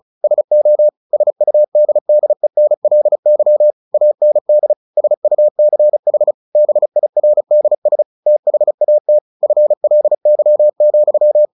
Morse code, heard as a keyed tone, about 34 wpm, 600 Hz. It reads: SO SUDDENLY AND SUCH BIRDS THAT FLY,
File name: the old man and the sea / the old man and the sea 1 35wpm